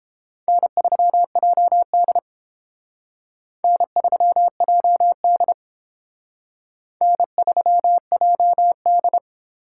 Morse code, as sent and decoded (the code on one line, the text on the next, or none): -. ...-- .--- -.. / -. ...-- .--- -.. / -. ...-- .--- -..
N3JD N3JD N3JD